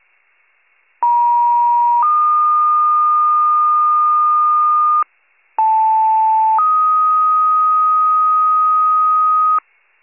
Two-tone page, heard as a one-second tone, then a three-second tone, about 950 then 1220 Hz, about 880 then 1250 Hz. It is heard from 1.0 to 9.6 s.